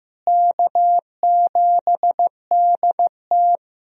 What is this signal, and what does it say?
Morse code, heard as a keyed tone, about 15 wpm, 700 Hz. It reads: K7DT
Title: USA Call Signs - 4 Characters - Repeated 2x - 15wpm